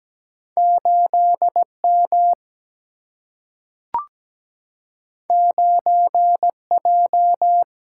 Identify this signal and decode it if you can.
Morse code, heard as a keyed tone, about 17 wpm, 700 Hz. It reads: 8 M 9 J
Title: Single-Letter Single-Number 17wpm